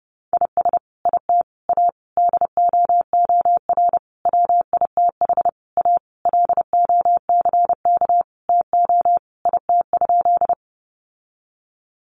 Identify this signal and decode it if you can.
Morse code, heard as a keyed tone, about 30 wpm, 700 Hz. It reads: IS IT A DOOR WITH A LOCK TO IT?
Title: Sentences from Top 1300 Words 30wpm